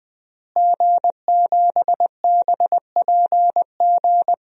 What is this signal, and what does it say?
Morse code, heard as a keyed tone, about 20 wpm, 700 Hz. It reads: G7BPG
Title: DXCC-100 Call Signs - 5 Characters - Repeated 2x - 20wpm